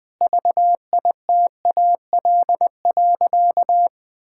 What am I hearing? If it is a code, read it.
Morse code, heard as a keyed tone, about 21 wpm, 700 Hz. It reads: VITAL.
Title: Sentences with 8-Word Phrases - 20wpm